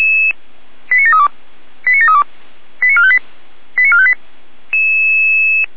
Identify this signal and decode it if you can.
Five-tone selective call (ZVEI), heard as quick runs of five tones, tones of about 70 ms. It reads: 89732 89732 89458 89458
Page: SDR/ZVEI